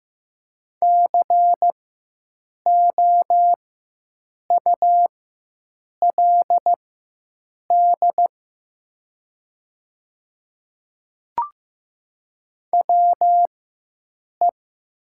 Morse code, heard as a keyed tone, about 15 wpm, spaced as for 7 wpm, 700 Hz. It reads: COULD WE
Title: 4x Character Spacing - Top 200 Words - 15wpm